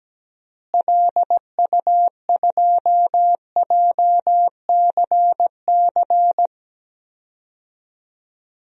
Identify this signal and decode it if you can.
Morse code, heard as a keyed tone, about 17 wpm, 700 Hz. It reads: LU2JCC
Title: DXCC-100 Call Signs - 6 Characters - Repeated 2x - 17wpm